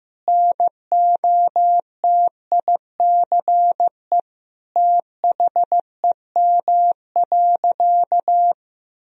Morse code, heard as a keyed tone, about 15 wpm, 700 Hz. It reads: NOTICE THEM.
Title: Sentences from Top 1000 Words - Review Part 2 - 15wpm